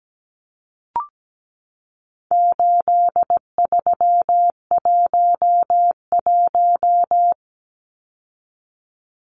Morse code, heard as a keyed tone, about 17 wpm, 700 Hz. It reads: 8311
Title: Numbers - 4-Digit 17wpm